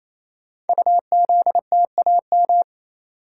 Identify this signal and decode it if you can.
Morse code, heard as a keyed tone, about 27 wpm, 700 Hz. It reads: UZTAM
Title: Code Groups - 5 Letters - 28wpm